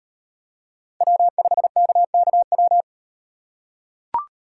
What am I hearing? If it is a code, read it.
Morse code, heard as a keyed tone, about 38 wpm, 700 Hz. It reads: W5KKW